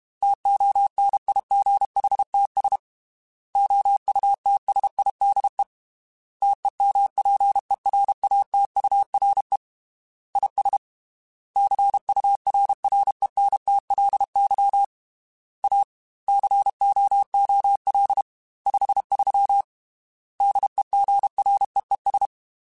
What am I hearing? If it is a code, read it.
Morse code, heard as a keyed tone, about 32 wpm, 800 Hz. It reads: TONIGHTS OUTSIDE TEMPERATURE IS CURRENTLY A COOL 53 DEGREES
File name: Weather Sets / Weather Sets-mp3s / Weather 5 1 @30